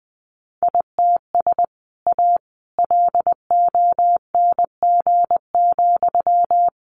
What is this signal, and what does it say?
Morse code, heard as a keyed tone, about 20 wpm, 700 Hz. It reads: ITS A LONG,
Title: Sentences from Top 600 Words 20wpm